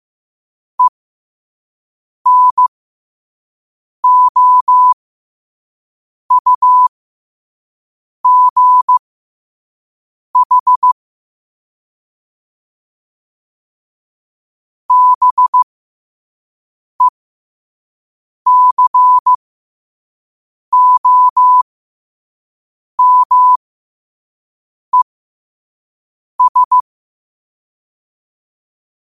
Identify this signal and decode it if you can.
Morse code, heard as a keyed tone, about 15 wpm, spaced as for 5 wpm, 1000 Hz. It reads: ENOUGH BECOMES
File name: qotdfiles / 5wpm